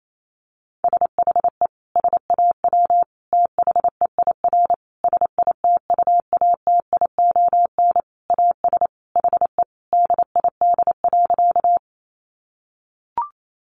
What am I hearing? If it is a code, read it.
Morse code, heard as a keyed tone, about 28 wpm, 700 Hz. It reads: SHE SAW THEIR SITUATION AS HE DID.
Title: Sentences from Top 1200 Words 28wpm